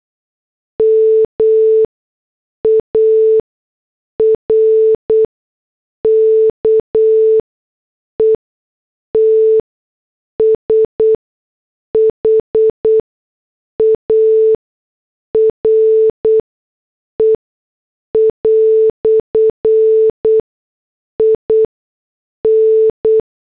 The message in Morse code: MARKET SHARE" IN